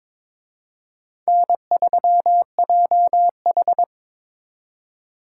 Morse code, heard as a keyed tone, about 26 wpm, 700 Hz. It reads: N3JH
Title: Speed Racing - USA Call Signs 22wpm